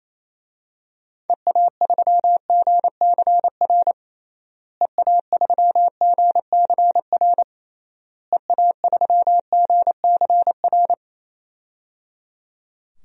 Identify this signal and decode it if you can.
Morse code, heard as a keyed tone, about 28 wpm, 700 Hz. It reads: EA3GCR EA3GCR EA3GCR
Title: DXCC-100 Call Signs - 6 Characters - Repeated 3x - 28wpm